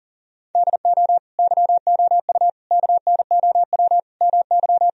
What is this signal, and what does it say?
Morse code, heard as a keyed tone, about 40 wpm, 700 Hz. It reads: DO YOU KNOW MY
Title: Sentences from Top 1000 Words - Review Part 3 - 40wpm